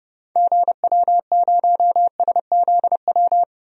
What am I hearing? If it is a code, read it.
Morse code, heard as a keyed tone, about 30 wpm, 700 Hz. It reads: GW0SZW
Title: DXCC-100 Call Signs - 6 Characters - 30wpm